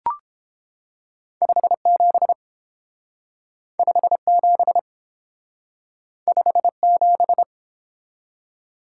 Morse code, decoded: 57 57 57

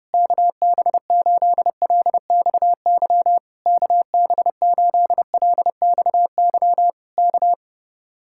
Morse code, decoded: KB8LXY KB8LXY K

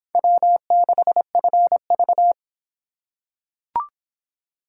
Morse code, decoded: W6FV

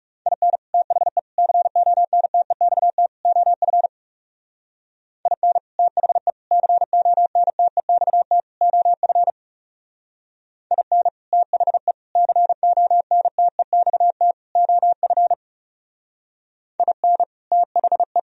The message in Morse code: IN THE CONTEXT OF IN THE CONTEXT OF IN THE CONTEXT OF IN THE

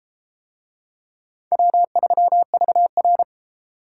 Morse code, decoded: W3VR

